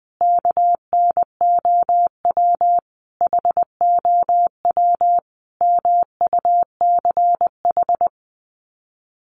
Morse code, heard as a keyed tone, about 20 wpm, 700 Hz. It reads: KNOW HOW MUCH
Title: Sets of 5 Words Encore 20wpm